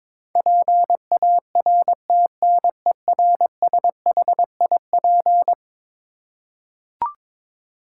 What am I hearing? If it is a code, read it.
Morse code, heard as a keyed tone, about 22 wpm, 700 Hz. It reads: PARTNERSHIP
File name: practice / 22wpm / Eleven Letter Words - 22wpm